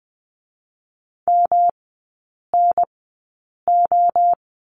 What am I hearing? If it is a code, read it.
Morse code, heard as a keyed tone, about 20 wpm, spaced as for 8 wpm, 700 Hz. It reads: MNO
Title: 152 Lesson 25e - Letter p - Three Characters 20wpm